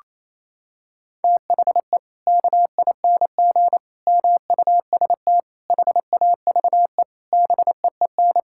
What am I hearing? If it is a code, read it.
Morse code, heard as a keyed tone, about 28 wpm, 700 Hz. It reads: THE KING MUST HAVE BEEN
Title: Sentences from Top 1000 Words - Review Part 5 - 28wpm